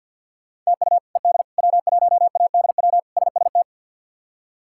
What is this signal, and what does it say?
Morse code, heard as a keyed tone, about 47 wpm, 700 Hz. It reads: TU ED W1ADW SST
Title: K1USN SST QSO Practice - 50wpm